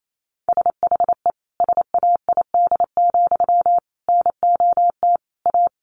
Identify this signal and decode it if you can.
Morse code, heard as a keyed tone, about 28 wpm, 700 Hz. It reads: SHE SAID, NOT A
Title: Sentences from Top 800 Words 28wpm